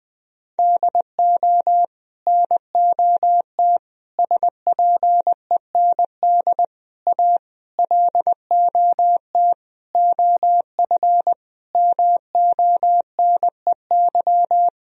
Morse code, heard as a keyed tone, about 21 wpm, 700 Hz. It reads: DO NOT SPEND A LOT OF MONEY
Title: Sentences from Top 1000 Words - Review Part 4 - 20wpm